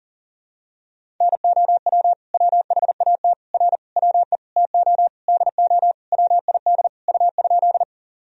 Morse code, heard as a keyed tone, about 40 wpm, 700 Hz. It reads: NOW WHAT R WE TO DO WID U?
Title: Sentences from Top 100 Words and Abbreviations 40wpm